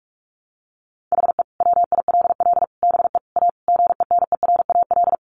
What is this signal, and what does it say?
Morse code, heard as a keyed tone, about 44 wpm, 700 Hz. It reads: HE WILL BE A GENERAL
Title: Sentences from Top 1200 Words 45wpm